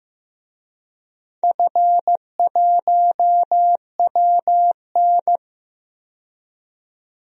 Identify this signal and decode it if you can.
Morse code, heard as a keyed tone, about 15 wpm, 700 Hz. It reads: F1WN
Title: DXCC-100 Call Signs - 4 Characters - 15wpm